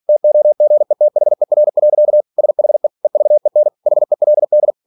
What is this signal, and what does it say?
Morse code, heard as a keyed tone, about 46 wpm, 600 Hz. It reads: TOGETHER. SHE EVEN HELD
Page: Never 47wpm